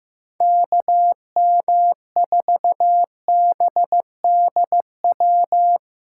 Morse code, heard as a keyed tone, about 17 wpm, 700 Hz. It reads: KM4BDW
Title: Speed Racing - USA Call Signs 15wpm